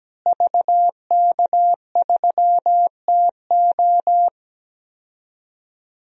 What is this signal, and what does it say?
Morse code, heard as a keyed tone, about 17 wpm, 700 Hz. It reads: VK3TO
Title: DXCC-100 Call Signs - 5 Characters - Repeated 3x - 17wpm